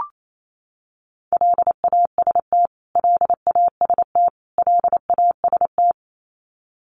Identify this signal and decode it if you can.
Morse code, heard as a keyed tone, about 28 wpm, 700 Hz. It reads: LAST LAST LAST
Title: Top 500 Words Repeated 28wpm